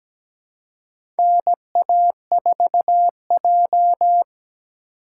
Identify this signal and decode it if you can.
Morse code, heard as a keyed tone, about 17 wpm, 700 Hz. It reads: NA4J
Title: USA Call Signs - 4x Character Spacing - 17wpm